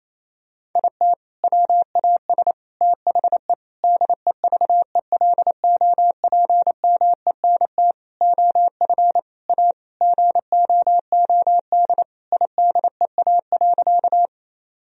Morse code, heard as a keyed tone, about 28 wpm, 700 Hz. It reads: IT WAS THE DEVELOPMENT OF A GOOD IDEA.
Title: Sentences from Top 1000 Words - Review Part 4 - 28wpm